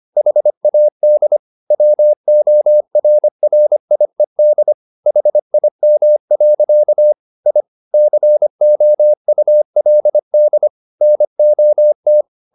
Morse code, heard as a keyed tone, about 25 wpm, 600 Hz. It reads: HAD WORRIED HIM. I COULD NOT